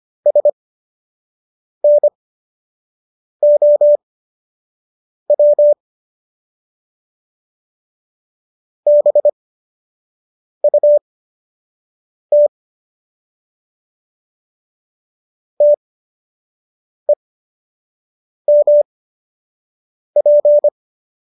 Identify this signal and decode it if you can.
Morse code, heard as a keyed tone, about 25 wpm, spaced as for 6 wpm, 590 Hz. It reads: SNOW BUT TEMP